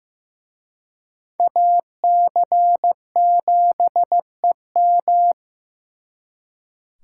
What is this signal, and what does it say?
Morse code, heard as a keyed tone, about 18 wpm, 700 Hz. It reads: AC7EM
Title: Speed Racing - USA 2x2 Call Signs 15wpm